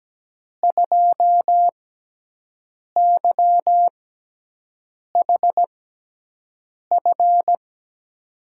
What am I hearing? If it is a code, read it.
Morse code, heard as a keyed tone, about 17 wpm, spaced as for 6 wpm, 700 Hz. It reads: 2YHF
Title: USA Call Signs - 5 Characters - 6x Character Spacing - 17wpm